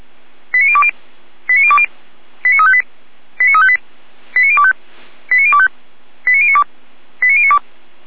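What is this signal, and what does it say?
Five-tone selective call (ZVEI), heard as quick runs of five tones, tones of about 70 ms. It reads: 89020 89020 89358 89358 89025 89025 89092 89092
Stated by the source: Received by SDR; ZVEI five-tone code 89270.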